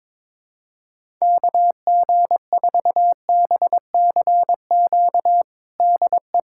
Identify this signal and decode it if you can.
Morse code, heard as a keyed tone, about 21 wpm, 700 Hz. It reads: KG4BCQ DE
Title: USA Callsign DE Callsign - 6 Characters - 22wpm